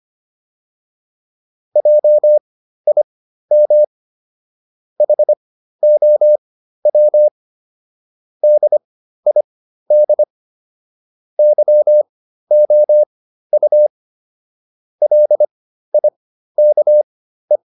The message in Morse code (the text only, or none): JIM HOW DID YOU LIKE